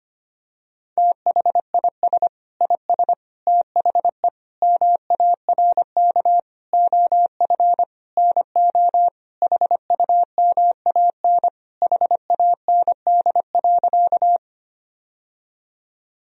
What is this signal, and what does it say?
Morse code, heard as a keyed tone, about 25 wpm, 700 Hz. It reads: THIS IS THE MARK OF NO HUMAN HAND.
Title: Sentences from Top 1000 Words - Review Part 3 - 25wpm